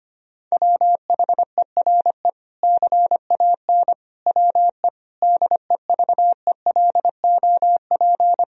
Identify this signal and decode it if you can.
Morse code, heard as a keyed tone, about 25 wpm, 700 Hz. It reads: WHERE CAN WE DEVELOP
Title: Sentences from Top 1000 Words - Review Part 2 - 25wpm